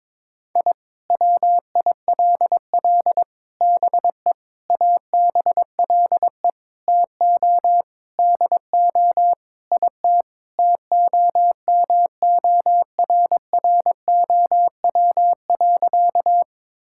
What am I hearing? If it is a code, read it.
Morse code, heard as a keyed tone, about 22 wpm, 700 Hz. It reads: I WILL BE ABLE TO DO IT TOMORROW.
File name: practice / 22wpm / Sentences with 4-Word Phrases - 22wpm